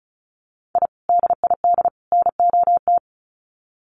Morse code, heard as a keyed tone, about 34 wpm, 700 Hz. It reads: I DID NOT
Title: Sets of 3 Words 35wpm